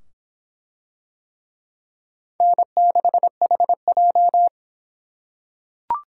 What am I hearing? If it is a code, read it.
Morse code, heard as a keyed tone, about 23 wpm, 700 Hz. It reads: N6HJ